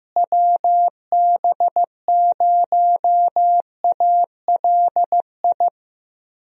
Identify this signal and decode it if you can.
Morse code, heard as a keyed tone, about 15 wpm, 700 Hz. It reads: WB0ALI
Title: USA Call Signs - 6 Characters - Repeated 2x - 15wpm